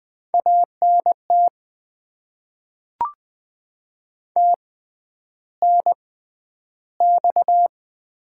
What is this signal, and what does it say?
Morse code, heard as a keyed tone, about 20 wpm, 700 Hz. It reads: A N T T N X
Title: Top 20 QSO Elements - 6x Character Spacing - 20wpm